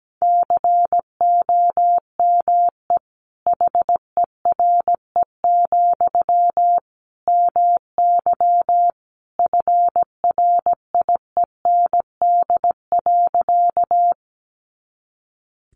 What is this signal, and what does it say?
Morse code, heard as a keyed tone, about 17 wpm, 700 Hz. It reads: COME HERE, MY FRIEND.
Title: Sentences from Top 300 Words 17wpm